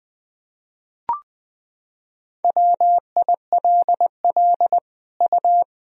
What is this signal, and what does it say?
Morse code, heard as a keyed tone, about 21 wpm, 700 Hz. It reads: WILL U
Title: Sentences from Top 500 Words and Abbreviations 20wpm